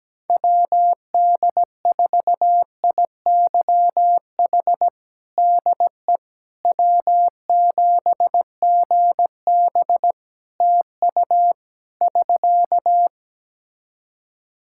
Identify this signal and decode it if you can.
Morse code, heard as a keyed tone, about 17 wpm, 700 Hz. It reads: WD4IYH DE W7GB TU <SK>
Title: Practice QSO - Exchange 4 - Ending - 17wpm